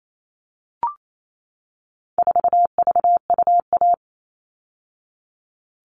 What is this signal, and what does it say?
Morse code, heard as a keyed tone, about 28 wpm, 700 Hz. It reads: 4VUA
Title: All Cut Numbers - 4-Digit 28wpm